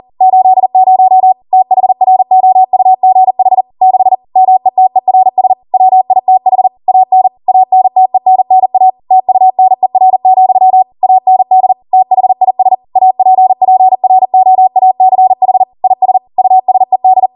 Morse code, heard as a keyed tone, about 40 wpm, 750 Hz. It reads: 80 THROUGH 6 METERS WITH AN ANTENNA TUNER, AND THIS APPROACH IS USED